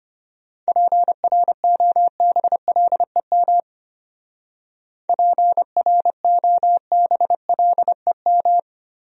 Morse code, heard as a keyed tone, about 31 wpm, 700 Hz. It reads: PROBLEM PROBLEM
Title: Speed-Racing Top 500 Words 25wpm